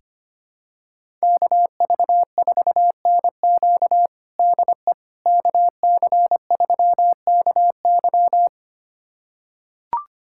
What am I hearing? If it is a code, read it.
Morse code, heard as a keyed tone, about 25 wpm, 700 Hz. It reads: KV4NQ DE KC3KY
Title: USA Callsign DE Callsign 25wpm